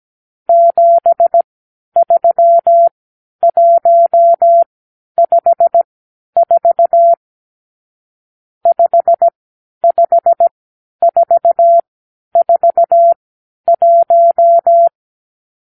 Morse code, heard as a keyed tone, about 17 wpm, spaced as for 10 wpm, 680 Hz. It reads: 73154 55441
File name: k7qo / Lesson 54 (9)